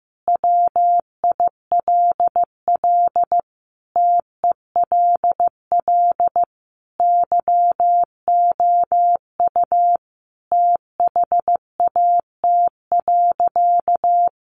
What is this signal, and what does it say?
Morse code, heard as a keyed tone, about 15 wpm, 700 Hz. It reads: WILL TELL YOU THAT.
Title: Sentences from Top 300 Words 15wpm